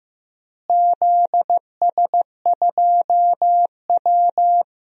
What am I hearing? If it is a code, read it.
Morse code, heard as a keyed tone, about 15 wpm, 700 Hz. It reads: ZS2W
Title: DXCC-100 Call Signs - 4 Characters - Repeated 2x - 15wpm